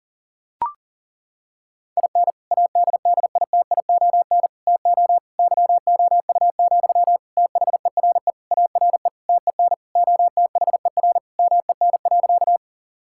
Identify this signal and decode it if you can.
Morse code, heard as a keyed tone, about 40 wpm, 690 Hz. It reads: IN ADDITION TO YOU, THERE ARE TEN OTHER MEN.
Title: Sentences from Top 1000 Words - Review Part 6 - 40wpm